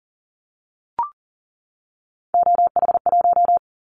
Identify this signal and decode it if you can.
Morse code, heard as a keyed tone, about 40 wpm, 700 Hz. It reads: OH1
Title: Callsign Prefixes for Top 25 DXCC - 40wpm